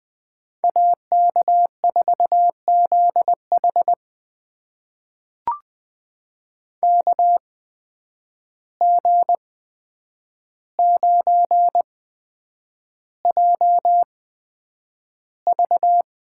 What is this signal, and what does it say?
Morse code, heard as a keyed tone, about 20 wpm, spaced as for 9 wpm, 700 Hz. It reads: AK4ZH KG9JV
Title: USA Call Signs - 5 Characters - 8x Character Spacing - 20wpm